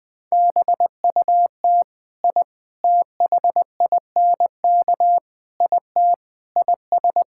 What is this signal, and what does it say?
Morse code, heard as a keyed tone, about 20 wpm, 700 Hz. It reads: BUT I THINK IT IS
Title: Sentences with 4-Word Phrases - 20wpm